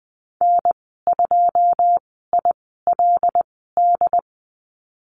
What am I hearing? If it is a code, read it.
Morse code, heard as a keyed tone, about 20 wpm, spaced as for 15 wpm, 700 Hz. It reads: N2ILD
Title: 082 Lesson 14g - Number 2 - Call Signs 20wpm